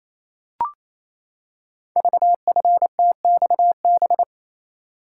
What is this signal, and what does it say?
Morse code, heard as a keyed tone, about 27 wpm, 690 Hz. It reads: VFTXB
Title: Code Groups - 5 Letters - 28wpm